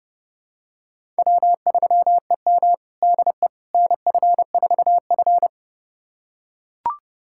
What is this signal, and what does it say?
Morse code, heard as a keyed tone, about 32 wpm, 700 Hz. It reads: W3EM DE NF4F
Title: USA Callsign DE Callsign - 4 Characters - 30wpm